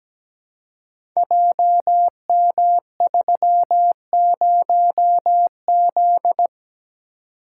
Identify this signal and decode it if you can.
Morse code, heard as a keyed tone, about 17 wpm, 700 Hz. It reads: JM30Z
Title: Code Groups - 5 Letter:Numbers - 17wpm